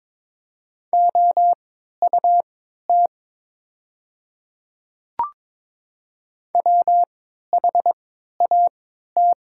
Morse code, heard as a keyed tone, about 22 wpm, spaced as for 12 wpm, 700 Hz. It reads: OUT WHAT
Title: 3x Character Spacing - Top 500 Words - 22wpm